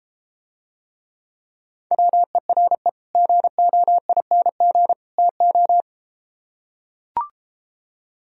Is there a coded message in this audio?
Yes: Morse code from 1.9 to 5.8 s, heard as a keyed tone, about 33 wpm, 700 Hz. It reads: WERE GOING TO